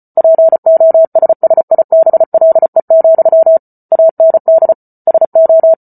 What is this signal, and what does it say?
Morse code, heard as a keyed tone, about 34 wpm, 650 Hz. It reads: POSSIBLE, AND SO